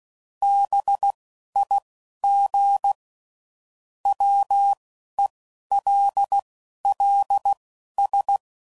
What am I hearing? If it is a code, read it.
Morse code, heard as a keyed tone, about 16 wpm, spaced as for 12 wpm, 800 Hz. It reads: BIG WELLS
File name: CW Code Groups @10WPM / fromSM's / Cities